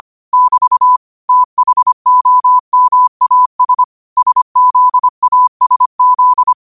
Morse code, heard as a keyed tone, about 25 wpm, 1000 Hz. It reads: <BT> THOMAS SZASZ
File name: qotdfiles / 25wpm